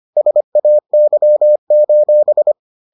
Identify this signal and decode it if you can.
Morse code, heard as a keyed tone, about 25 wpm, 600 Hz. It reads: SAY: